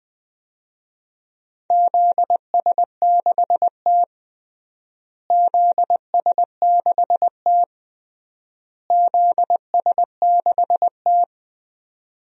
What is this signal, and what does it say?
Morse code, heard as a keyed tone, about 20 wpm, 700 Hz. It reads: ZS6T ZS6T ZS6T
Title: DXCC-100 Call Signs - 4 Characters - Repeated 3x - 20wpm